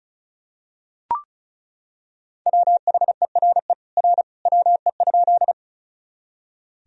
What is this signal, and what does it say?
Morse code, heard as a keyed tone, about 34 wpm, 690 Hz. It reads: WHERE R WE?
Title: Sentences from Top 200 Words and Abbreviations 35wpm